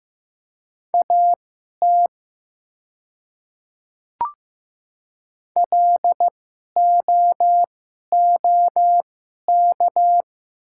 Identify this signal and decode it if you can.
Morse code, heard as a keyed tone, about 15 wpm, spaced as for 11 wpm, 700 Hz. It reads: AT LOOK
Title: 2x Character Spacing - Top 200 Words - 15wpm